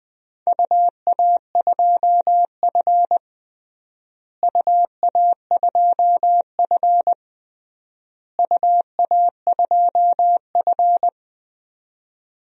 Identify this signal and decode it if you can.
Morse code, heard as a keyed tone, about 20 wpm, 700 Hz. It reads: UA2F UA2F UA2F
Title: DXCC-100 Call Signs - 4 Characters - Repeated 3x - 20wpm